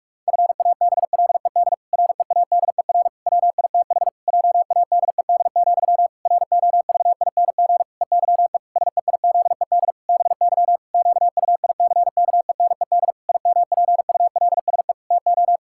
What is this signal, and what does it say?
Morse code, heard as a keyed tone, about 46 wpm, 700 Hz. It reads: FABLED READER WITH JADED, ROVING EYE SEIZED BY QUICKENED IMPULSE TO
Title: Pangram Challenge v2 - 45wpm